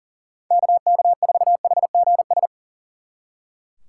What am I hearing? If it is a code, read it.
Morse code, heard as a keyed tone, about 50 wpm, 700 Hz. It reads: KK4HGS
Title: Speed Racing - USA 2x3 Call Signs 40wpm